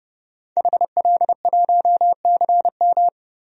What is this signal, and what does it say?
Morse code, heard as a keyed tone, about 31 wpm, 700 Hz. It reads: HL1CM
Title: DXCC-100 Call Signs - 5 Characters - 30wpm